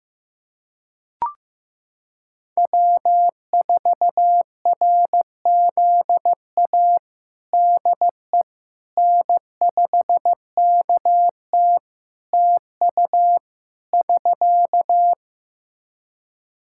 Morse code, heard as a keyed tone, about 15 wpm, 700 Hz. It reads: W4RZA DE N5KT TU <SK>
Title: Practice QSO - Exchange 4 - Ending - 15wpm